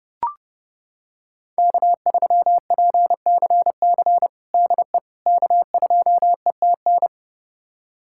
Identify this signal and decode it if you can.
Morse code, heard as a keyed tone, about 30 wpm, 700 Hz. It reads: K3PCC DE K2ETN
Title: USA Callsign DE Callsign - 5 Characters - 30wpm